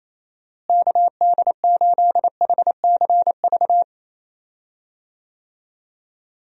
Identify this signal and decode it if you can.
Morse code, heard as a keyed tone, about 28 wpm, 700 Hz. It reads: KD8HCV
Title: USA Call Signs - 6 Characters - Repeated 2x - 28wpm